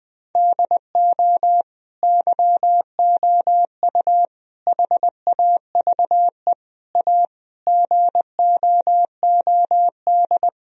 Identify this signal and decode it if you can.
Morse code, heard as a keyed tone, about 20 wpm, 700 Hz. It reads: DO YOU HAVE A GOOD
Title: Sentences from Top 1000 Words - Review Part 3 - 20wpm